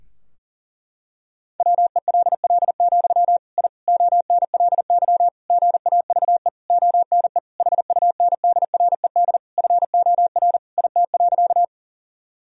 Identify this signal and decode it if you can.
Morse code, heard as a keyed tone, about 40 wpm, 700 Hz. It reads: WELL, I ONLY GAVE ONE HUNDRED FOR IT.
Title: Sentences from Top 1000 Words - Review Part 2 - 40wpm